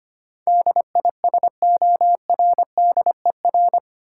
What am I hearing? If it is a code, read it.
Morse code, heard as a keyed tone, about 25 wpm, 700 Hz. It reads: DISORDER
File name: 8 Letter Words - First 3 Letters Given - 25wpm